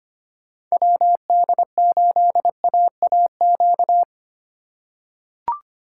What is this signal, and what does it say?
Morse code, heard as a keyed tone, about 25 wpm, 700 Hz. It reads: WD8AAQ
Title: USA Call Signs - 6 Characters - 3x Character Spacing - 25wpm